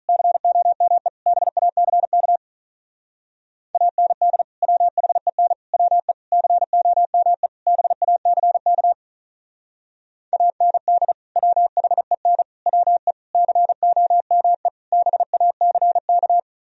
Morse code, heard as a keyed tone, about 43 wpm, 700 Hz. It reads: COME BACK AND WHEN WE COME BACK AND WHEN WE COME BACK
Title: Speed-Racing - Sets of 5 Words 35wpm